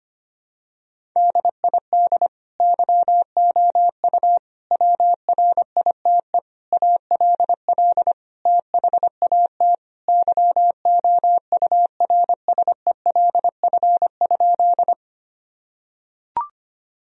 Morse code, heard as a keyed tone, about 25 wpm, 700 Hz. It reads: DID YOU WRITE ALL THAT YOURSELF?
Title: Sentences from Top 1000 Words - Review Part 1 - 25wpm